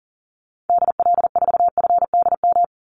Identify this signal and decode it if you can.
Morse code, heard as a keyed tone, about 40 wpm, 700 Hz. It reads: DL4FDM
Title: Call Signs Repeated 2x - 40wpm